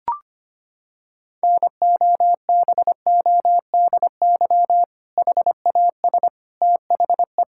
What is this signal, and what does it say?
Morse code, heard as a keyed tone, about 25 wpm, 700 Hz. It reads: NOBODY HAS THE
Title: Sentences from Top 1600 Words 25wpm